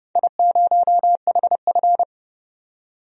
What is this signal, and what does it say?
Morse code, heard as a keyed tone, about 30 wpm, 700 Hz. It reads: I0HF
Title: DXCC-100 Call Signs - 4 Characters - 30wpm